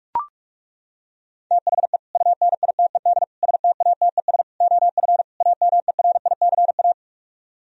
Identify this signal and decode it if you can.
Morse code, heard as a keyed tone, about 40 wpm, 700 Hz. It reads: THE UNITED STATES OF AMERICA